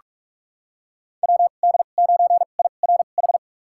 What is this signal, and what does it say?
Morse code, heard as a keyed tone, about 43 wpm, spaced as for 32 wpm, 690 Hz. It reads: WD9IRH
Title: USA Call Signs - 6 Characters - 2x Character Spacing - 45wpm